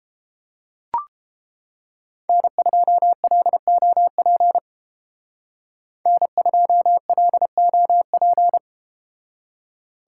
Morse code, heard as a keyed tone, about 31 wpm, 700 Hz. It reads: N2LOP N2LOP